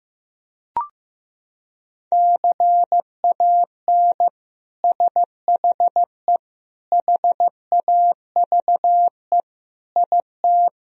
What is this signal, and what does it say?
Morse code, heard as a keyed tone, about 16 wpm, 700 Hz. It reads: CAN SHE HAVE IT
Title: Sentences from Top 1000 Words - Review Part 4 - 15wpm